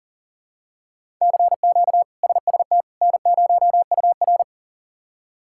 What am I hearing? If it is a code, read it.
Morse code, heard as a keyed tone, about 37 wpm, 700 Hz. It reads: CQ SST N0UR
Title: K1USN SST QSO Practice - 40wpm